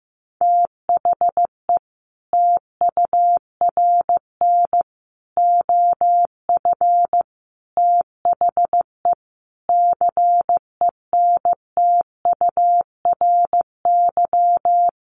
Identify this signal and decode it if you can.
Morse code, heard as a keyed tone, about 15 wpm, 700 Hz. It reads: THE TURN OF THE CENTURY
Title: Sets of 5 Words 15wpm